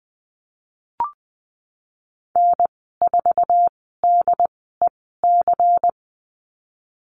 Morse code, heard as a keyed tone, about 20 wpm, spaced as for 15 wpm, 700 Hz. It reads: N4DEC
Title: 231 Lesson 37g - Letter x - Call Signs 20wpm